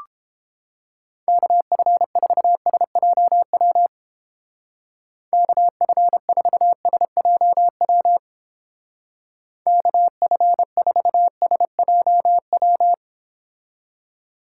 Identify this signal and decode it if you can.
Morse code, heard as a keyed tone, about 29 wpm, 700 Hz. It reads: KF4SJW KF4SJW KF4SJW